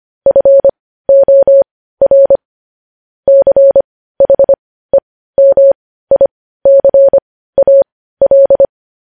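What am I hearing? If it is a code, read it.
Morse code, heard as a keyed tone, about 25 wpm, spaced as for 15 wpm, 550 Hz. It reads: FOR CHEMICAL
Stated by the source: Morse code news headlines sent at 15.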